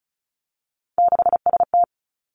Morse code, heard as a keyed tone, about 36 wpm, 700 Hz. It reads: 6ST
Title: Three Challenging Letter-Number Combinations 35wpm